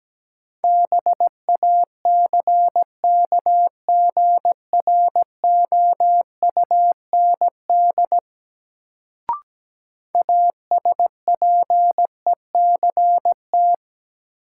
Morse code, heard as a keyed tone, about 17 wpm, 700 Hz. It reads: BACKGROUND ASPECT
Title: Sentences from Top 1300 Words - New Word Drill - 17wpm